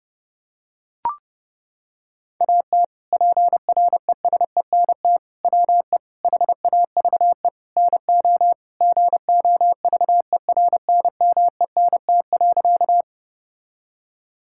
Morse code, heard as a keyed tone, about 31 wpm, 700 Hz. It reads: AT PRESENT WE HAVE NO GOVERNMENT.
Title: Sentences from Top 1000 Words - Review Part 5 - 30wpm